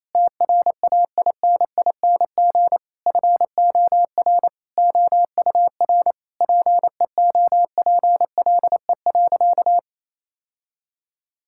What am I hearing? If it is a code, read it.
Morse code, heard as a keyed tone, about 28 wpm, 700 Hz. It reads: TRAINING FOR OUR PEOPLE.
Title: Sentences from Top 1000 Words - Review Part 5 - 28wpm